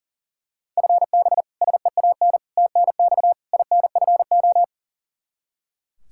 Morse code, heard as a keyed tone, about 40 wpm, 700 Hz. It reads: FB SEAN TNX INFO
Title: Practice QSO - Exchange 3 - More Info - 40wpm